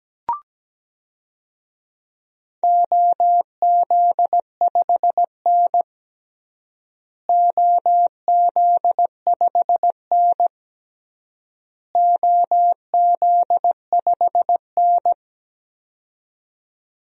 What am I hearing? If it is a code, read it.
Morse code, heard as a keyed tone, about 17 wpm, 700 Hz. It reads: OZ5N OZ5N OZ5N
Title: DXCC-100 Call Signs - 4 Characters - Repeated 3x - 17wpm